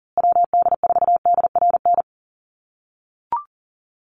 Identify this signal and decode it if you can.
Morse code, heard as a keyed tone, about 40 wpm, 700 Hz. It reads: WD4DRN
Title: USA Call Signs Repeated 2x 40wpm